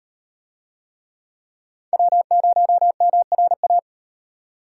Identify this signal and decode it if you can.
Morse code, heard as a keyed tone, about 36 wpm, 700 Hz. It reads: W0MRA